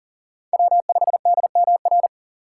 Morse code, heard as a keyed tone, about 40 wpm, 700 Hz. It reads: W5DMR
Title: USA Call Signs - 5 Characters - 3x Character Spacing - 40wpm